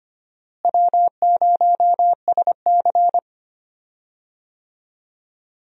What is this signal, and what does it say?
Morse code, heard as a keyed tone, about 25 wpm, 700 Hz. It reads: W0SC
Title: USA Call Signs - 4 Characters - Repeated 3x - 25wpm